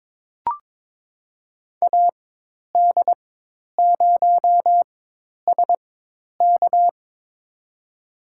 Morse code, heard as a keyed tone, about 23 wpm, spaced as for 10 wpm, 700 Hz. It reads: AD0SK
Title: USA Call Signs - 5 Characters - 4x Character Spacing - 22wpm